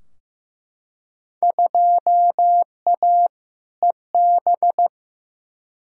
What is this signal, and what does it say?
Morse code, heard as a keyed tone, about 15 wpm, 700 Hz. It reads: 2A EB
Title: Field Day Exchanges - 15wpm